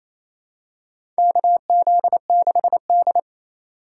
Morse code, heard as a keyed tone, about 28 wpm, 700 Hz. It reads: KZ6D
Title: USA Call Signs - 4 Characters - 3x Character Spacing - 28wpm